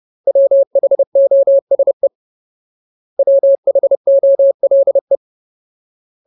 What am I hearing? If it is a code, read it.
Morse code, heard as a keyed tone, about 30 wpm, 550 Hz. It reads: WHOSE WHOLE
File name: Ultimate Word List / 30WPM Ultimate